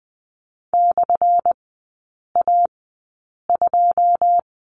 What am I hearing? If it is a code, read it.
Morse code, heard as a keyed tone, about 20 wpm, spaced as for 8 wpm, 700 Hz. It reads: /A2
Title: 191 Lesson 31e - Letter v - Three Characters 20wpm